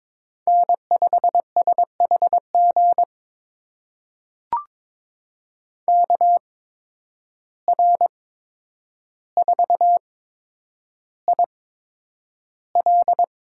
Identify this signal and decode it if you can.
Morse code, heard as a keyed tone, about 22 wpm, spaced as for 9 wpm, 700 Hz. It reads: N5SHG KR4IL
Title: USA Call Signs - 5 Characters - 8x Character Spacing - 22wpm